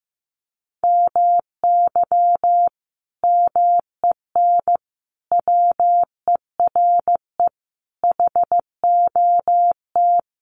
Morse code, heard as a keyed tone, about 15 wpm, 700 Hz. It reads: MY MEN WERE HOT
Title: Sentences from Top 1100 Words 15wpm